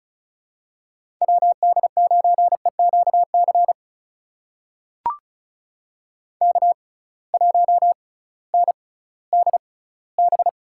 Morse code, heard as a keyed tone, about 36 wpm, spaced as for 19 wpm, 700 Hz. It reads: WD9EQC K1NDB
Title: USA Call Signs - 6x Character Spacing - 35wpm